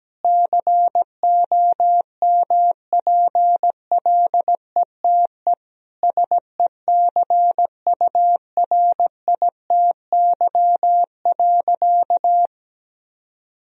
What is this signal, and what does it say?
Morse code, heard as a keyed tone, about 17 wpm, 700 Hz. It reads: COMPLETE SECURITY.
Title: Sentences from Top 1000 Words - Review Part 4 - 17wpm